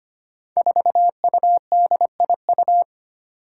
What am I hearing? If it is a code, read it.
Morse code, heard as a keyed tone, about 25 wpm, 700 Hz. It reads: 4UDIU
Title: Five Challenging Letter-Number Combinations 25wpm